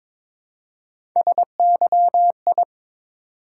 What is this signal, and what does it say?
Morse code, heard as a keyed tone, about 22 wpm, 700 Hz. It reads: SYI
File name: practice / 22wpm / Three Letter-Number Combinations - 22wpm